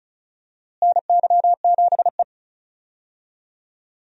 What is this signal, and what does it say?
Morse code, heard as a keyed tone, about 36 wpm, 700 Hz. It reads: NY7E
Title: USA Call Signs - 4 Characters - Repeated 2x - 35wpm